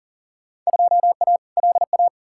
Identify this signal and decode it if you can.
Morse code, heard as a keyed tone, about 40 wpm, 700 Hz. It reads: 2A LA
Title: Field Day Exchanges - 40wpm